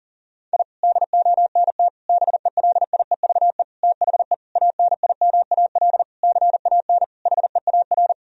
Morse code, heard as a keyed tone, about 37 wpm, 700 Hz. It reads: I DONT BELIEVE THE ANIMAL CAN HEAR
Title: Sentences from Top 1000 Words - Review Part 1 - 40wpm